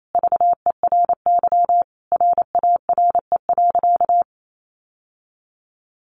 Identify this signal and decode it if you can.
Morse code, heard as a keyed tone, about 28 wpm, 700 Hz. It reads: VERY RARE.
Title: Sentences from Top 1100 Words 28wpm